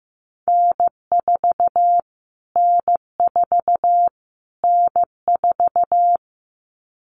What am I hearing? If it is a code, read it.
Morse code, heard as a keyed tone, about 15 wpm, 700 Hz. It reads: N4 N4 N4
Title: Callsign Prefixes for Top 25 DXCC - Repeated 3x - 15wpm